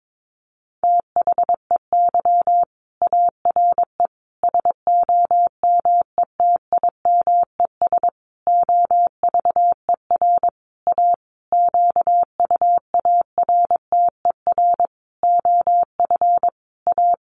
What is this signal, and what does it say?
Morse code, heard as a keyed tone, about 22 wpm, 700 Hz. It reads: THEY ARE SOMETIMES OVER A QUARTER OF A
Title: Sentences from Top 1300 Words 22wpm